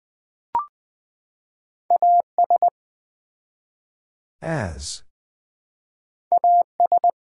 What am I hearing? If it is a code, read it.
Morse code, heard as a keyed tone, about 20 wpm, 700 Hz. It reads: AS AS
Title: Build A Phrase - 3-Word Phrases - 20wpm